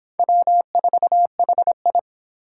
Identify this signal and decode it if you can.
Morse code, heard as a keyed tone, about 26 wpm, 700 Hz. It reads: W4HI